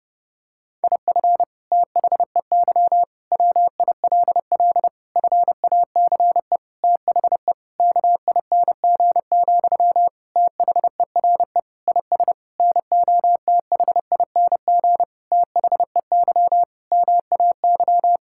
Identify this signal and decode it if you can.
Morse code, heard as a keyed tone, about 29 wpm, 700 Hz. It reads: IF THEY WILL FACE THE KING, THERE IS NOTHING THEY MAY
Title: Sentences from Top 1000 Words - Review Part 6 - 30wpm